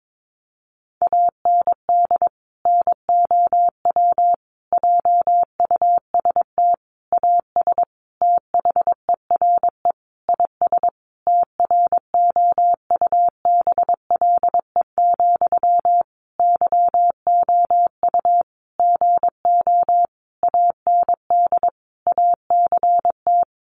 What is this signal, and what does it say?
Morse code, heard as a keyed tone, about 22 wpm, 700 Hz. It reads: AND NOW JUST AS THERE IS TROUBLE, YOU GO AND ACT
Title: Sentences from Top 1200 Words 22wpm